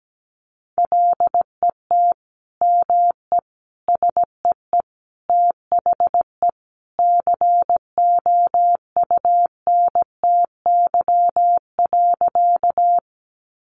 Morse code, heard as a keyed tone, about 17 wpm, 700 Hz. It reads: LET ME SEE THE COUNTY.
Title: Sentences from Top 1200 Words 17wpm